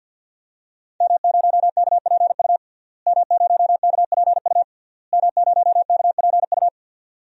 Morse code, heard as a keyed tone, about 47 wpm, 700 Hz. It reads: M0KPU M0KPU M0KPU
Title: DXCC-100 Call Signs - 5 Characters - Repeated 3x - 50wpm